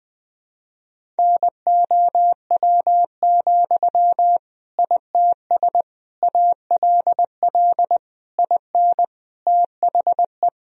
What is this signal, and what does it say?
Morse code, heard as a keyed tone, about 21 wpm, 700 Hz. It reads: NOW, ITS ALL IN THE
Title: Sentences from Top 1000 Words - Review Part 3 - 20wpm